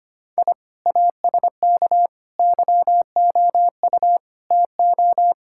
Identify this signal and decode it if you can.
Morse code, heard as a keyed tone, about 25 wpm, 700 Hz. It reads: I ASK YOU TO
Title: Sentences from Top 1000 Words - Review Part 5 - 25wpm